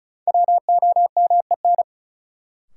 Morse code, heard as a keyed tone, about 36 wpm, 700 Hz. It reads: WOMEN
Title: Sets of 3 Words - New Word Drill - 35wpm